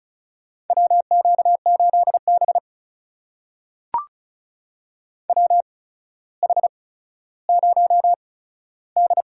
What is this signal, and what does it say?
Morse code, heard as a keyed tone, about 36 wpm, spaced as for 15 wpm, 700 Hz. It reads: WQ8B WH0D
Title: USA Call Signs - 4 Characters - 8x Character Spacing - 35wpm